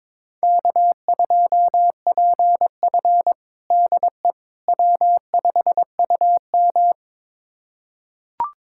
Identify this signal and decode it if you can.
Morse code, heard as a keyed tone, about 22 wpm, 700 Hz. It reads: K2PF DE W5UM
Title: USA Callsign DE Callsign - 4 Characters - 22wpm